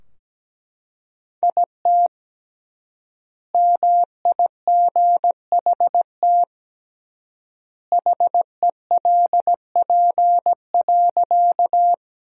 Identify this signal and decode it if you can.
Morse code, heard as a keyed tone, about 17 wpm, 700 Hz. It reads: IT MIGHT HELP.
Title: 3x Word-Spacing - Sentences from Top 300 Words - 17wpm